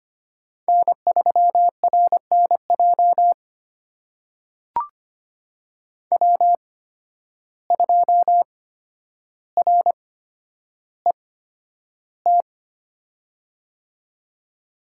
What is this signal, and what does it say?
Morse code, heard as a keyed tone, about 25 wpm, spaced as for 11 wpm, 700 Hz. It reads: N3RNJ W2RET
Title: USA Call Signs - 5 Characters - 8x Character Spacing - 25wpm